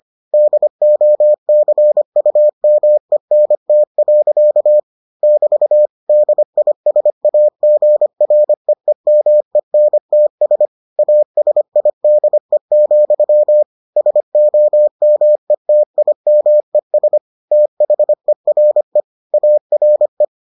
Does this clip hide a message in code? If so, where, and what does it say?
Yes: Morse code from 0.3 to 20.2 s, heard as a keyed tone, about 25 wpm, 600 Hz. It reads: DOCUMENT. <BT> DISAGREEMENTS ASIDE, SOMETIMES THERE ARE